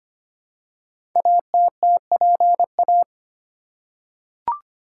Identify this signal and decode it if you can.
Morse code, heard as a keyed tone, about 25 wpm, 700 Hz. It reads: ATTPA
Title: Code Groups - 5 Letters - 25wpm